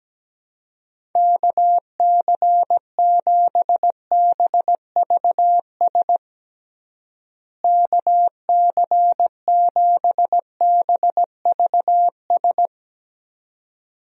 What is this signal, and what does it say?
Morse code, heard as a keyed tone, about 17 wpm, 700 Hz. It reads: KC7BVS KC7BVS
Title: USA Call Signs - 6 Characters - Repeated 2x - 17wpm